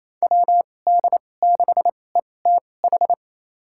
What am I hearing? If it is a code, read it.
Morse code, heard as a keyed tone, about 28 wpm, spaced as for 20 wpm, 700 Hz. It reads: WD6ETH
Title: USA Call Signs - 2x Character Spacing - 28wpm